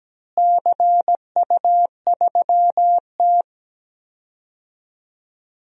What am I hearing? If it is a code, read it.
Morse code, heard as a keyed tone, about 17 wpm, 700 Hz. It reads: CU3T
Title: DXCC-100 Call Signs - 4 Characters - Repeated 3x - 17wpm